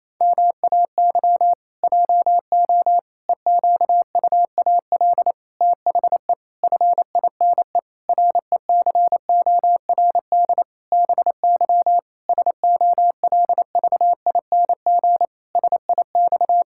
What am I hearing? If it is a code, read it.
Morse code, heard as a keyed tone, about 28 wpm, 700 Hz. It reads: MAY JO EQUAL THE FINE RECORD BY SOLVING SIX
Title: Pangram Challenge v2 - 28wpm